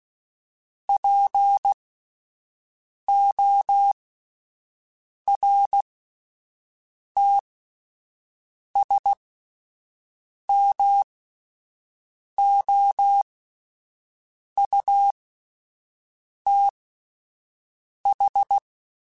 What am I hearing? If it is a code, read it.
Morse code, heard as a keyed tone, about 16 wpm, spaced as for 5 wpm, 800 Hz. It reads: PORTSMOUTH